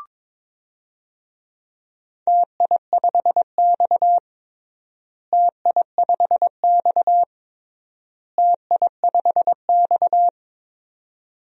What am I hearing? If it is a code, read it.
Morse code, heard as a keyed tone, about 22 wpm, 700 Hz. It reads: TI5X TI5X TI5X
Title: DXCC-100 Call Signs - 4 Characters - Repeated 3x - 22wpm